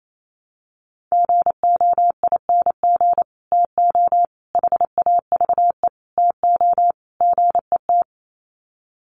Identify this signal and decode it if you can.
Morse code, heard as a keyed tone, about 28 wpm, 700 Hz. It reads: GOING TO HAVE TO GET
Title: Sets of 5 Words Encore2 28wpm